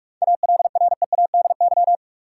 Morse code, heard as a keyed tone, about 41 wpm, 700 Hz. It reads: ALREADY
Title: Sentences from Top 700 Words - New Word Drill - 45wpm